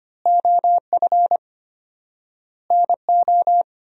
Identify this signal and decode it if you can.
Morse code, heard as a keyed tone, about 25 wpm, 700 Hz. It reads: OF NO
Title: 4x Word-Spacing - Sentences from Top 100 Words - 25wpm